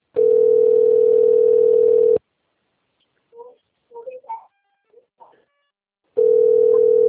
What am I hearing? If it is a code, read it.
Ringback tone (North America), 440 + 480 Hz, with speech.
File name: rg-1600-unknown- - - .11876